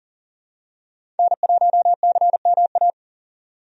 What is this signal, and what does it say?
Morse code, heard as a keyed tone, about 40 wpm, 700 Hz. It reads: N1CMA
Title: USA Call Signs - 5 Characters - 40wpm